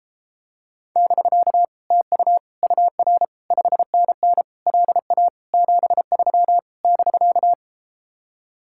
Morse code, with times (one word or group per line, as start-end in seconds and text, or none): 1.0-1.6 <BK>
1.9-2.4 TU
2.6-3.2 UR
3.5-4.4 5NN
4.7-5.3 LA
5.5-6.6 73
6.8-7.5 <BK>